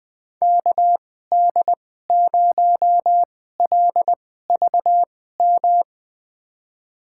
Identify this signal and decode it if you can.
Morse code, heard as a keyed tone, about 20 wpm, spaced as for 15 wpm, 700 Hz. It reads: KD0LVM
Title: USA Call Signs - 6 Characters - 2x Character Spacing - 20wpm